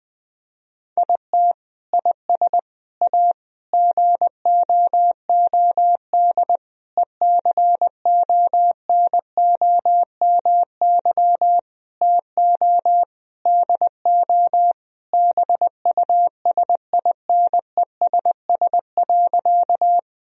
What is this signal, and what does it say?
Morse code, heard as a keyed tone, about 20 wpm, 700 Hz. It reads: IT IS A GOOD ECONOMY TO DO BUSINESS.
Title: Sentences from Top 1000 Words - Review Part 3 - 20wpm